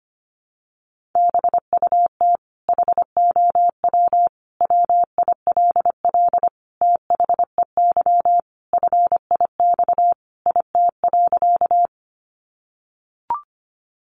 Morse code, heard as a keyed tone, about 25 wpm, 700 Hz. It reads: BUT HOW WILL THEY FIX IT.
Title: Sentences from Top 1400 Words 25wpm